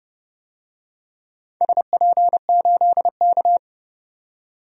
Morse code, heard as a keyed tone, about 32 wpm, 700 Hz. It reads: SP8K
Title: DXCC-100 Call Signs - 4 Characters - 30wpm